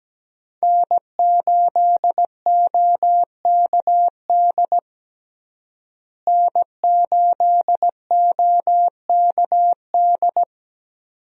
Morse code, heard as a keyed tone, about 17 wpm, 700 Hz. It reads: N8OKD N8OKD
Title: USA Call Signs - 5 Characters - Repeated 2x - 17wpm